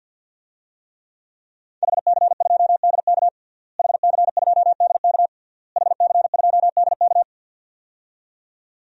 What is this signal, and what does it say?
Morse code, heard as a keyed tone, about 52 wpm, 700 Hz. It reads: HC2DK HC2DK HC2DK
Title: DXCC-100 Call Signs - 5 Characters - Repeated 3x - 50wpm